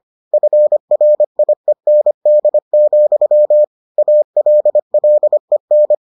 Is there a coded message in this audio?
Yes: Morse code from 0.3 to 6.0 s, heard as a keyed tone, about 25 wpm, 600 Hz. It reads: FRIEND, ALLEN